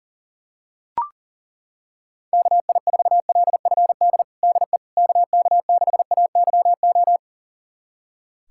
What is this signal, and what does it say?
Morse code, heard as a keyed tone, about 40 wpm, 700 Hz. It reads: KI4LFD DE KK6AYO
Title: USA Callsign DE Callsign 40wpm